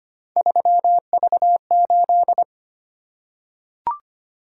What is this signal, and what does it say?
Morse code, heard as a keyed tone, about 25 wpm, 700 Hz. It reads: 3V8
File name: Three Letter-Number Combinations - 25wpm